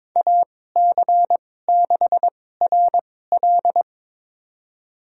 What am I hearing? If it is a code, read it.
Morse code, heard as a keyed tone, about 22 wpm, spaced as for 16 wpm, 700 Hz. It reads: AC6RL
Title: USA Call Signs - 5 Characters - 2x Character Spacing - 22wpm